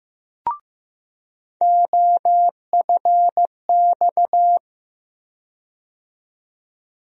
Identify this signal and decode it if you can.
Morse code, heard as a keyed tone, about 15 wpm, 700 Hz. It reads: OFX
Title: USA Call Sign Postfixes - 3 Letters 15wpm